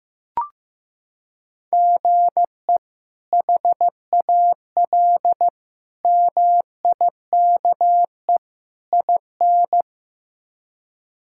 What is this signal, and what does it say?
Morse code, heard as a keyed tone, about 16 wpm, 700 Hz. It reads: GE HAL MIKE IN
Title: K1USN SST QSO Practice - 15wpm